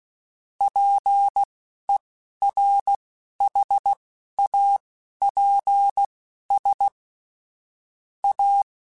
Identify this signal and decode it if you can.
Morse code, heard as a keyed tone, about 16 wpm, spaced as for 11 wpm, 800 Hz. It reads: PERHAPS A
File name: Technical Subjects / Technical Subjects-mp3s / Sw Pwr Sup @10